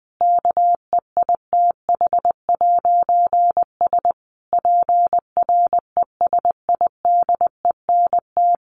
Morse code, heard as a keyed tone, about 20 wpm, 700 Hz. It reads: KEITH'S PRESIDENT
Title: A Sound Of Thunder 20wpm